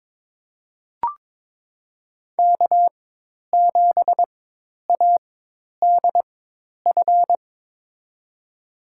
Morse code, heard as a keyed tone, about 22 wpm, spaced as for 10 wpm, 700 Hz. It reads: K7ADF
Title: USA Call Signs - 5 Characters - 4x Character Spacing - 22wpm